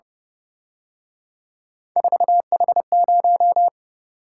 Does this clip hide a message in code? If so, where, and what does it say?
Yes: Morse code from 2.0 to 3.7 s, heard as a keyed tone, about 29 wpm, 700 Hz. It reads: 4H0